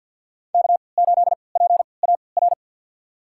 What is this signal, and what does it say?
Morse code, heard as a keyed tone, about 46 wpm, spaced as for 27 wpm, 700 Hz. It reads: K8PAR
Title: USA Call Signs - 5 Characters - 3x Character Spacing - 50wpm